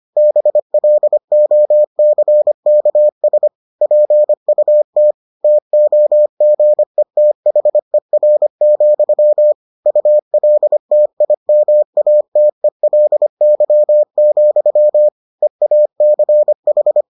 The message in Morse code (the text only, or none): BLOCKS PUT TOGETHER, ULTIMATELY, EACH